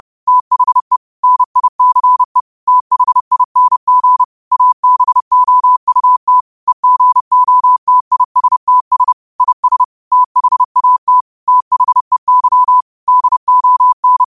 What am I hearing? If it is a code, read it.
Morse code, heard as a keyed tone, about 30 wpm, 1000 Hz. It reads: THE NICE THING ABOUT EGOTISTS IS THAT THEY DON
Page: Quotes for Wed, 13 Aug 2025 in Morse Code at 30 words per minute.